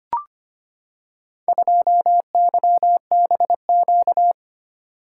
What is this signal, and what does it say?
Morse code, heard as a keyed tone, about 25 wpm, 700 Hz. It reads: 2YBQ
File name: practice / 25wpm / Four Letter-Number Combinations - 25wpm